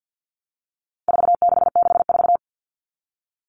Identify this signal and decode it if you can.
Morse code, heard as a keyed tone, about 50 wpm, 700 Hz. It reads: 4664
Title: Numbers - 4-Digit 50wpm